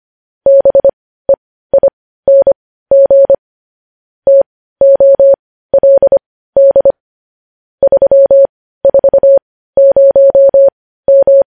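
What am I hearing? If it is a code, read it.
Morse code, heard as a keyed tone, about 25 wpm, spaced as for 15 wpm, 550 Hz. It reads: BEING TOLD 340M
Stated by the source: Morse code news headlines sent at 15.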